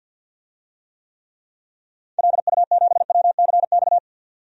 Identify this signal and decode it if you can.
Morse code, heard as a keyed tone, about 49 wpm, 700 Hz. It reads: LU7WCX
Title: DXCC-100 Call Signs - 6 Characters - 50wpm